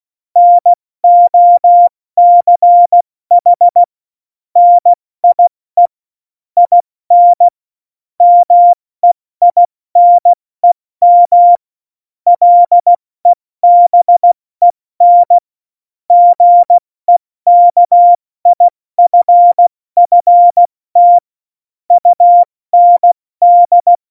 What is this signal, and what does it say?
Morse code, heard as a keyed tone, about 16 wpm, 700 Hz. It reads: NOCH NIE IN MEINEM LEBEN GEKIFFT UND
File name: Morsekurs / Witze / Witz 08. Dezember 2020, 70 F80